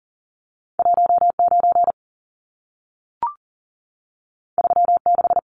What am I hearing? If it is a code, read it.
Morse code, heard as a keyed tone, about 40 wpm, 700 Hz. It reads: 19 36
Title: Numbers - 2-Digit 40wpm